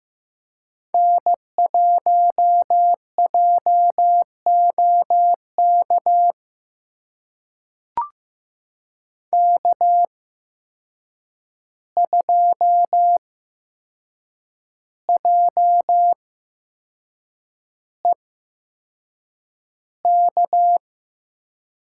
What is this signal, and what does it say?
Morse code, heard as a keyed tone, about 15 wpm, spaced as for 6 wpm, 700 Hz. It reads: N1JOK K2JEK
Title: USA Call Signs - 5 Characters - 8x Character Spacing - 15wpm